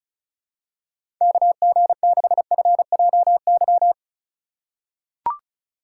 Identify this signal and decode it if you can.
Morse code, heard as a keyed tone, about 36 wpm, 700 Hz. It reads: KG6FJY
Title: USA Call Signs - 6 Characters - 2x Character Spacing - 35wpm